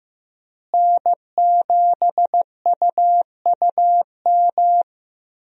Morse code, heard as a keyed tone, about 15 wpm, 700 Hz. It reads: N7UUM
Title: USA Call Signs - 5 Characters - Repeated 3x - 15wpm